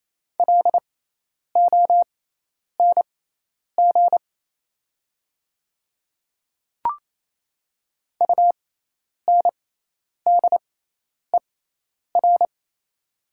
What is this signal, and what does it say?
Morse code, heard as a keyed tone, about 28 wpm, spaced as for 10 wpm, 700 Hz. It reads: LONG UNDER
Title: 6x Character Spacing - Top 200 Words - 28wpm